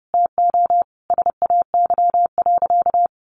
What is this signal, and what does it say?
Morse code, heard as a keyed tone, about 30 wpm, 700 Hz. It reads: TO SAY.
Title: Sentences from Top 600 Words 30wpm